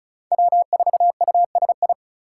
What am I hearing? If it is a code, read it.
Morse code, heard as a keyed tone, about 34 wpm, 700 Hz. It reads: W4USI
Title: K1USN SST QSO Practice - 35wpm